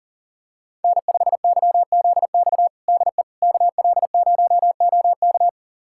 Morse code, heard as a keyed tone, about 40 wpm, 700 Hz. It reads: N5YZX DE KL0OK
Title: USA Callsign DE Callsign - 5 Characters - 40wpm